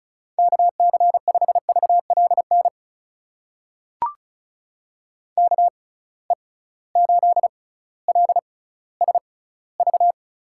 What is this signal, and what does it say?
Morse code, heard as a keyed tone, about 36 wpm, spaced as for 18 wpm, 690 Hz. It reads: KC5VLN KE8LSV
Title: USA Call Signs - 6 Characters - 6x Character Spacing - 35wpm